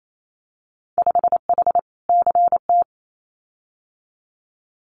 Morse code, heard as a keyed tone, about 28 wpm, 700 Hz. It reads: 5H CT
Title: Winter Field Day Exchanges Encore 28wpm